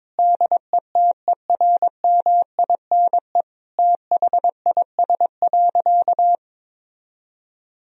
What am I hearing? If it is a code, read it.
Morse code, heard as a keyed tone, about 22 wpm, 700 Hz. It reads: DETERMINE THIS.
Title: Sentences from Top 1000 Words - Review Part 4 - 22wpm